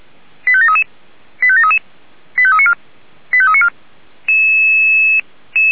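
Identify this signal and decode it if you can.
Five-tone selective call (ZVEI), heard as quick runs of five tones, tones of about 70 ms. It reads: 85733 85733 85394 85394
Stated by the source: ZVEI five-tone selective call received by SDR on a server of the Feuerwehr Orlamünde fire brigade.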